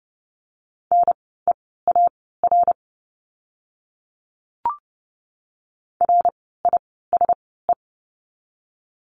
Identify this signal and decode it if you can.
Morse code, heard as a keyed tone, about 30 wpm, spaced as for 17 wpm, 700 Hz. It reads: NEAR RISE
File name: Lesson 16f - Letter c - Words 30wpm